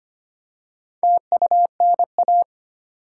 Morse code, heard as a keyed tone, about 25 wpm, 690 Hz. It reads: TUNA
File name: practice / 25wpm / Easy Words with Hints - 25wpm